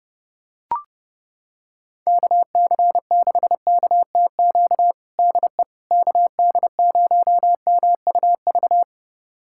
Morse code, heard as a keyed tone, about 30 wpm, 700 Hz. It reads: KC6KTQ DE KD0MUV
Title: USA Callsign DE Callsign 30wpm